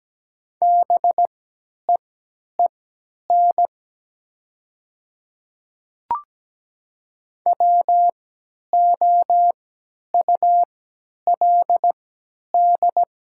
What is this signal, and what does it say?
Morse code, heard as a keyed tone, about 17 wpm, spaced as for 10 wpm, 700 Hz. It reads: BEEN WOULD
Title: 3x Character Spacing - Top 200 Words - 17wpm